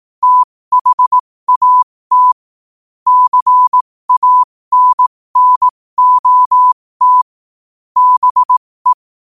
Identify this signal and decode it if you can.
Morse code, heard as a keyed tone, about 18 wpm, spaced as for 15 wpm, 1000 Hz. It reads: THAT CANNOT BE
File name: qotdfiles / 15wpm